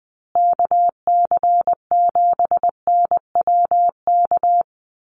Morse code, heard as a keyed tone, about 20 wpm, 700 Hz. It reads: KC7NWK
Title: USA Call Signs Repeated 3x 20wpm